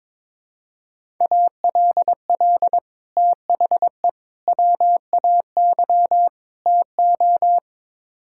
Morse code, heard as a keyed tone, about 22 wpm, 700 Hz. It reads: ALL THE WAY TO
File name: practice / 22wpm / Build A Phrase - 5-Word Phrases - 22wpm